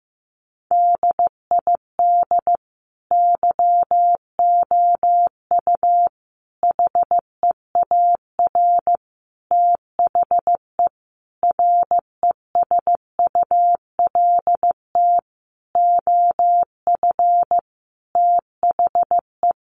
Morse code, heard as a keyed tone, about 15 wpm, 700 Hz. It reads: DID YOU HEAR THE RESULT OF THE
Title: Sentences from Top 1400 Words 15wpm